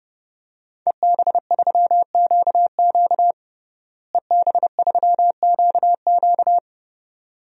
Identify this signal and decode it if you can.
Morse code, heard as a keyed tone, about 32 wpm, 700 Hz. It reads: EB3QQ EB3QQ
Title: DXCC-100 Call Signs - 5 Characters - Repeated 2x - 30wpm